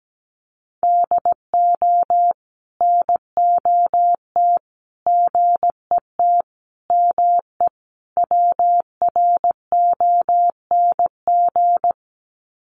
Morse code, heard as a keyed tone, about 17 wpm, 700 Hz. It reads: DO NOT GET ME WRONG
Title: Sets of 5 Words Encore 17wpm